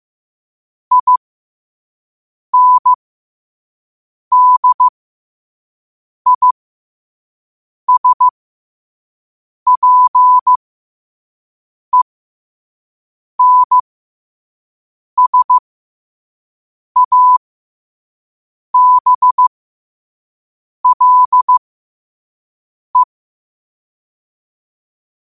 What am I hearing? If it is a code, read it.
Morse code, heard as a keyed tone, about 15 wpm, spaced as for 5 wpm, 1000 Hz. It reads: INDISPENSABLE
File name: qotdfiles / 5wpm